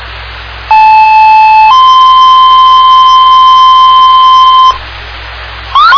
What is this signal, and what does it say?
Two-tone page, heard as a one-second tone, then a three-second tone, about 870 then 1040 Hz. It is heard from 0.7 to 4.7 s.
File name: Indian Lake Fire